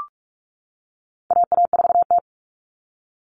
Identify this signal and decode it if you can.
Morse code, heard as a keyed tone, about 44 wpm, 700 Hz. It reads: AA4T
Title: All Cut Numbers - 4-Digit 45wpm